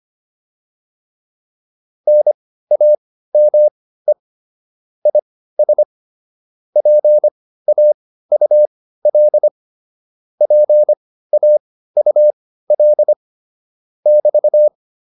Morse code, heard as a keyed tone, about 25 wpm, spaced as for 15 wpm, 600 Hz. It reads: NAME IS PAUL PAUL <BT>
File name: QSO403 15